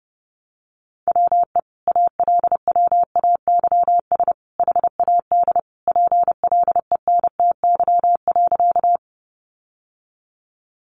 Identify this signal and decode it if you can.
Morse code, heard as a keyed tone, about 28 wpm, 700 Hz. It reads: WE ALWAYS HAD PLENTY.
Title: Sentences from Top 1300 Words 30wpm